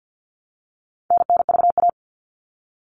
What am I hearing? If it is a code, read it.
Morse code, heard as a keyed tone, about 48 wpm, 700 Hz. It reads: NNVA
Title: All Cut Numbers - 4-Digit 50wpm